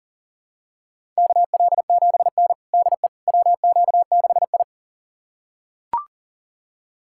Morse code, heard as a keyed tone, about 40 wpm, 700 Hz. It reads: KL7N DE WQ6I
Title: USA Callsign DE Callsign - 4 Characters - 40wpm